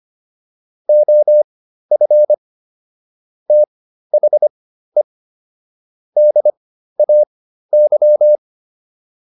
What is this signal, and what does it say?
Morse code, heard as a keyed tone, about 25 wpm, spaced as for 13 wpm, 600 Hz. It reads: OF THE DAY